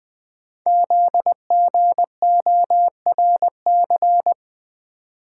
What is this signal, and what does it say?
Morse code, heard as a keyed tone, about 20 wpm, 700 Hz. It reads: ZGORC
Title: Code Groups - 5 Letters - 20wpm